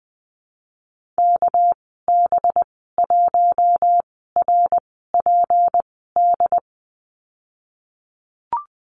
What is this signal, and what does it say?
Morse code, heard as a keyed tone, about 20 wpm, spaced as for 14 wpm, 700 Hz. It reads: KB1RPD
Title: 207 Lesson 33g - Letter j - Call Signs 20wpm